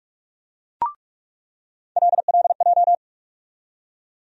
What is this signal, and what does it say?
Morse code, heard as a keyed tone, about 46 wpm, 700 Hz. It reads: LLJ
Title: USA Call Sign Postfixes - 3 Letters 45wpm